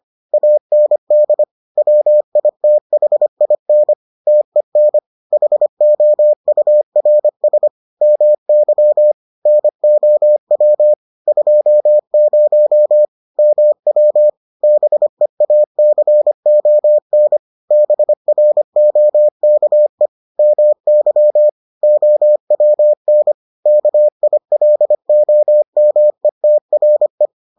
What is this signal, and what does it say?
Morse code, heard as a keyed tone, about 25 wpm, 600 Hz. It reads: AND WITHIN TEN HOURS MY NOW 20 MW BEACON BROKE MY OWN KILOMETRE